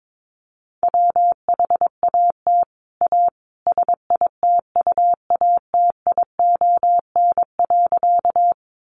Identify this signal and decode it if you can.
Morse code, heard as a keyed tone, about 22 wpm, 700 Hz. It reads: WHAT A SITUATION.
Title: Sentences from Top 1200 Words 22wpm